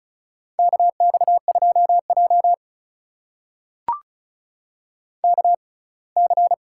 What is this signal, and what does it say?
Morse code, heard as a keyed tone, about 34 wpm, spaced as for 23 wpm, 700 Hz. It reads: KX2J KC
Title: USA Call Signs - 6x Character Spacing - 35wpm